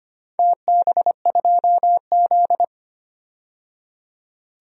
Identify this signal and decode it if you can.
Morse code, heard as a keyed tone, about 25 wpm, 700 Hz. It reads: TB2Z
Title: DXCC-100 Call Signs - 4 Characters - 25wpm